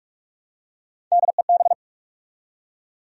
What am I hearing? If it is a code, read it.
Morse code, heard as a keyed tone, about 46 wpm, 700 Hz. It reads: DEB
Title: USA Common Names - Max 3 Letters - 45wpm